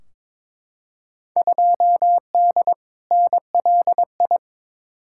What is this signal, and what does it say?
Morse code, heard as a keyed tone, about 22 wpm, 700 Hz. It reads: 2D NLI
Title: Field Day Exchanges - 22wpm